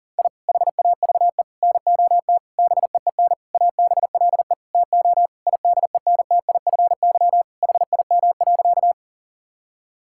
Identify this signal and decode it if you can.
Morse code, heard as a keyed tone, about 37 wpm, 700 Hz. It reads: I HAVE NOT BEEN ABLE TO IDENTIFY HIM.
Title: Sentences from Top 1000 Words - Review Part 6 - 40wpm